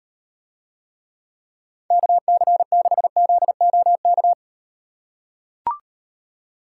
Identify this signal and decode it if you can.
Morse code, heard as a keyed tone, about 38 wpm, 700 Hz. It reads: KC6ZOK